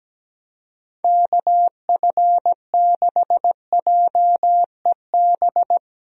Morse code, heard as a keyed tone, about 17 wpm, 700 Hz. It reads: KF6JEB
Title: USA Call Signs - 6 Characters - Repeated 2x - 17wpm